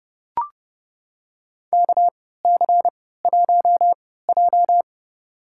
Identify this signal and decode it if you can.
Morse code, heard as a keyed tone, about 30 wpm, spaced as for 17 wpm, 700 Hz. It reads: KC1J
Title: USA Call Signs - 4 Characters - 3x Character Spacing - 30wpm